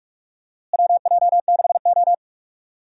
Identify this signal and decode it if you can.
Morse code, heard as a keyed tone, about 47 wpm, 700 Hz. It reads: WJ6O
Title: USA Call Signs - 4 Characters - Repeated 2x - 45wpm